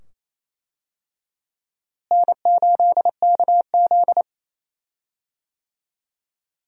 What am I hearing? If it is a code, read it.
Morse code, heard as a keyed tone, about 27 wpm, 700 Hz. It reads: N8KZ
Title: USA Call Signs - 4 Characters - Repeated 2x - 28wpm